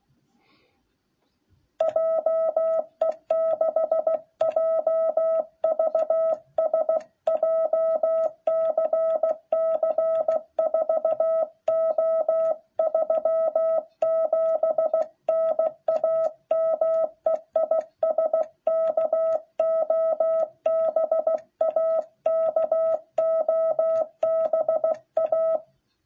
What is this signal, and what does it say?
Morse code, heard as a keyed tone, about 16 wpm, 650 Hz. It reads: JE6JVS JCC 4O37 NAME IS KOBA KOBA